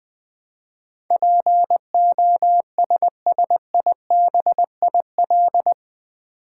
Morse code, heard as a keyed tone, about 20 wpm, 700 Hz. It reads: POSSIBIL
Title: Build A Word - 4 Letters At A Time - 11-Letter Words - 20wpm